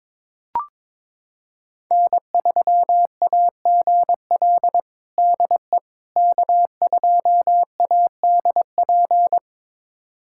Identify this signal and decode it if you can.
Morse code, heard as a keyed tone, about 22 wpm, 700 Hz. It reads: N3AGL DE K2ADP
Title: USA Callsign DE Callsign - 5 Characters - 22wpm